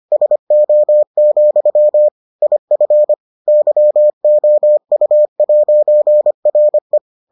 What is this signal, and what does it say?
Morse code, heard as a keyed tone, about 25 wpm, 600 Hz. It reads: SO, IF YOU'RE